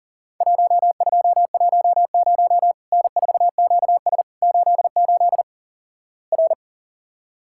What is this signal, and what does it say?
Morse code, heard as a keyed tone, about 37 wpm, 700 Hz. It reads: 1210 N4QS 88 R
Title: Worked All Europe DX Contest - QTC Exchange - 40wpm